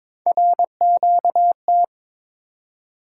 Morse code, heard as a keyed tone, about 22 wpm, 700 Hz. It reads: RQT
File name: practice / 22wpm / Three Letter-Number Combinations - 22wpm